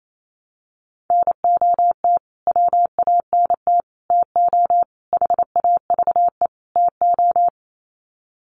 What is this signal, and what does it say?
Morse code, heard as a keyed tone, about 28 wpm, 700 Hz. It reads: NOT WANT TO HAVE TO
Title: Sets of 5 Words Encore2 28wpm